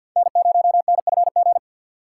Morse code, heard as a keyed tone, about 48 wpm, 700 Hz. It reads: N0NFG
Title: USA Call Signs - 5 Characters - 2x Character Spacing - 50wpm